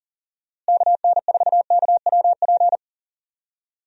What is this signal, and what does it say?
Morse code, heard as a keyed tone, about 40 wpm, 700 Hz. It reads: KN4KWP
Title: USA Call Signs - 6 Characters - Repeated 2x - 40wpm